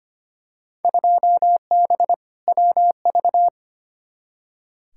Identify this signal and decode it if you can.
Morse code, heard as a keyed tone, about 25 wpm, 700 Hz. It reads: 2B WV
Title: Field Day Exchanges Encore - 25wpm